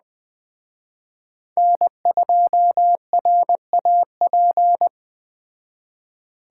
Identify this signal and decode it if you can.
Morse code, heard as a keyed tone, about 20 wpm, 700 Hz. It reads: N2RAP
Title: Speed Racing - USA Call Signs 17wpm